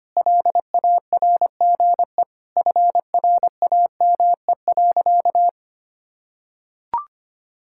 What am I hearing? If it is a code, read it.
Morse code, heard as a keyed tone, about 25 wpm, 700 Hz. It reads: LARGE FRAME.
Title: Sentences from Top 1000 Words - Review Part 6 - 25wpm